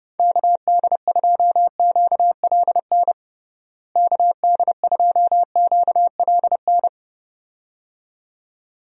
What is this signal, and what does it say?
Morse code, heard as a keyed tone, about 32 wpm, 700 Hz. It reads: KD2QLN KD2QLN
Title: USA Call Signs - 6 Characters - Repeated 2x - 30wpm